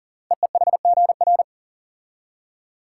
Morse code, heard as a keyed tone, about 40 wpm, 700 Hz. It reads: EEHGR
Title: Code Groups - 5 Letters - 40wpm